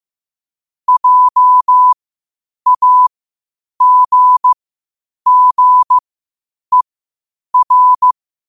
Morse code, heard as a keyed tone, about 15 wpm, spaced as for 8 wpm, 1000 Hz. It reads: JAGGER
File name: qotdfiles / 8wpm